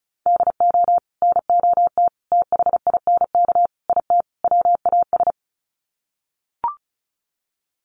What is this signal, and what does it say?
Morse code, heard as a keyed tone, about 34 wpm, 700 Hz. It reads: DO NOT THINK IT WAS
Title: Sets of 5 Words Encore 35wpm